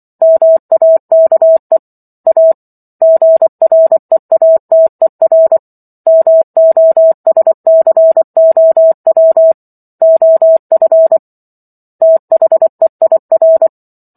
Morse code, heard as a keyed tone, about 24 wpm, 650 Hz. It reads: MAKE A GREATER MOSCOW OF THEIR
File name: Chapter 15 - 24 WPM